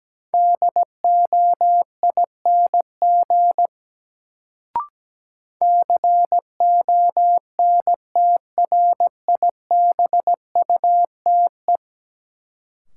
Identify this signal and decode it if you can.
Morse code, heard as a keyed tone, about 17 wpm, 700 Hz. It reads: DOING CONTRIBUTE
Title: Sentences from Top 1400 Words - New Word Drill - 17wpm